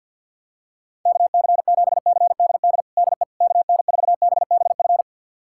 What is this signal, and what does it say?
Morse code, heard as a keyed tone, about 52 wpm, 700 Hz. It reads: KC7CDD DE KN4BBF
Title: USA Callsign DE Callsign - 6 Characters - 50wpm